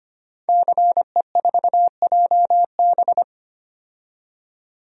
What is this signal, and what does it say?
Morse code, heard as a keyed tone, about 25 wpm, 700 Hz. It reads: CE4JB
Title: DXCC-100 Call Signs - 5 Characters - 25wpm